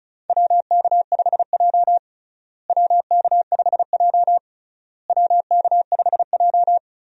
Morse code, heard as a keyed tone, about 36 wpm, 700 Hz. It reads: WK5J WK5J WK5J
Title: USA Call Signs - 4 Characters - Repeated 3x - 35wpm